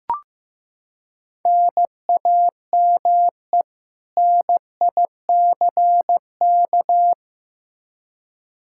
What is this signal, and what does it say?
Morse code, heard as a keyed tone, about 15 wpm, 700 Hz. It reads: NAME NICK
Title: Practice QSO - Exchange 1 - Beginning - 15wpm